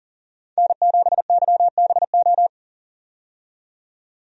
Morse code, heard as a keyed tone, about 40 wpm, 700 Hz. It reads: N7YBO
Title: USA Call Signs - 5 Characters - 40wpm